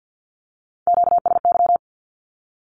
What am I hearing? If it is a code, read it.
Morse code, heard as a keyed tone, about 50 wpm, 700 Hz. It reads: QSY
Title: Q-Codes Repeated 3x 50wpm